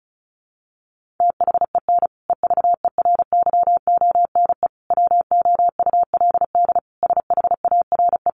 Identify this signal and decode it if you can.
Morse code, heard as a keyed tone, about 34 wpm, 700 Hz. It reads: THEN EVERYONE WOULD SHARE
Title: Sentences from Top 1400 Words 35wpm